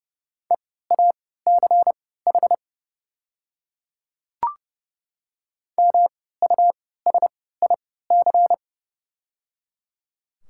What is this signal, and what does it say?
Morse code, heard as a keyed tone, about 30 wpm, spaced as for 17 wpm, 700 Hz. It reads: EACH MUSIC
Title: 3x Character Spacing - Top 300 Words - 30wpm